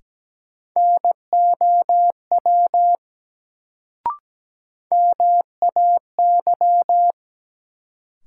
Morse code, heard as a keyed tone, about 17 wpm, 700 Hz. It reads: NOW MAY
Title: Sentences from Top 100 Words - New Word Drill - 17wpm